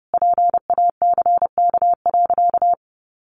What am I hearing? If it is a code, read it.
Morse code, heard as a keyed tone, about 30 wpm, 700 Hz. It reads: PACK.
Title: Sentences from Top 1300 Words 30wpm